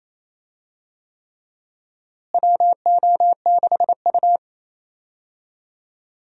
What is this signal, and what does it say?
Morse code, heard as a keyed tone, about 28 wpm, 700 Hz. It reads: WO6U
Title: USA Call Signs - 4 Characters - Repeated 3x - 28wpm